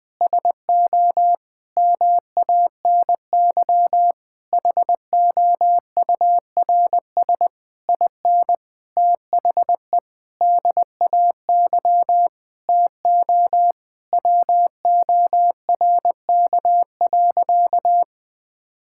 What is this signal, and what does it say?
Morse code, heard as a keyed tone, about 21 wpm, 700 Hz. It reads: SO MANY HOURS IN THE DAY TO WORK.
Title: Sentences with 9-Word Phrases - 20wpm